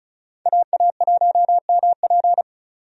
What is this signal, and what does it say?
Morse code, heard as a keyed tone, about 36 wpm, 700 Hz. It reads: AA1MP
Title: USA Call Signs - 5 Characters - 35wpm